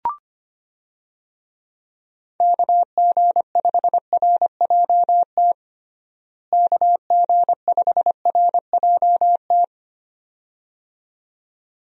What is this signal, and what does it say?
Morse code, heard as a keyed tone, about 25 wpm, 700 Hz. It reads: KG5RJT KG5RJT
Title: USA Call Signs - 6 Characters - Repeated 2x - 25wpm